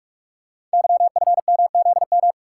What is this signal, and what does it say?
Morse code, heard as a keyed tone, about 44 wpm, 700 Hz. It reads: YFMZM
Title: Code Groups - 5 Letters - 45wpm